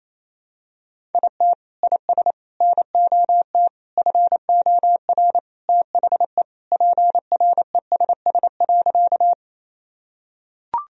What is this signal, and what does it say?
Morse code, heard as a keyed tone, about 28 wpm, 700 Hz. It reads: IT IS NOT FOR THE PRESS.
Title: Sentences from Top 1000 Words - Review Part 1 - 28wpm